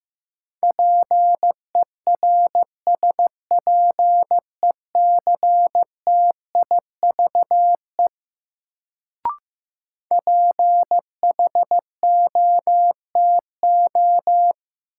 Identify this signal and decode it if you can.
Morse code, heard as a keyed tone, about 15 wpm, 700 Hz. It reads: PERSPECTIVE PHOTO
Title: Sentences from Top 1200 Words - New Word Drill Alt - 15wpm